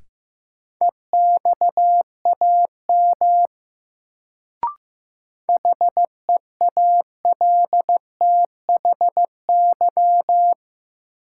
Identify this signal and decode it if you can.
Morse code, heard as a keyed tone, about 15 wpm, 700 Hz. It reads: EXAM HEALTHY
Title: Sentences from Top 1000 Words - Review Part 5 - New Word Drill - 15wpm